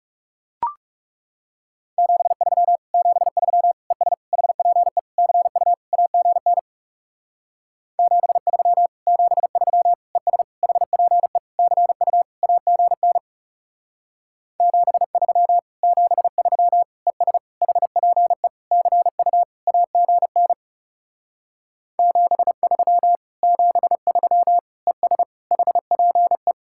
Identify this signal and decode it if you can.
Morse code, heard as a keyed tone, about 38 wpm, 700 Hz. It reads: 73 73 ES HPE CU AGN 73 73 ES HPE CU AGN 73 73 ES HPE CU AGN 73 73 ES HPE